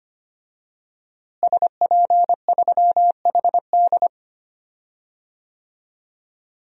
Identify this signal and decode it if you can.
Morse code, heard as a keyed tone, about 25 wpm, 700 Hz. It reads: SP3HD
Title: DXCC-100 Call Signs - 5 Characters - Repeated 3x - 25wpm